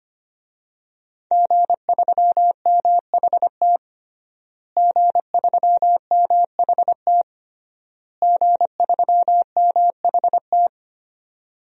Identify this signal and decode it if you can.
Morse code, heard as a keyed tone, about 25 wpm, 700 Hz. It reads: G3MHT G3MHT G3MHT
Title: DXCC-100 Call Signs - 5 Characters - Repeated 3x - 25wpm